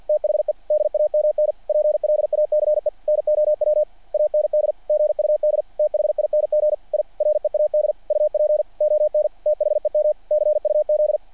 Morse code, heard as a keyed tone, about 48 wpm, 590 Hz. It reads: THE DAMN PLACE NOW AND GUD THING I READ UP ON THEM CUZ
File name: archived / CRSnet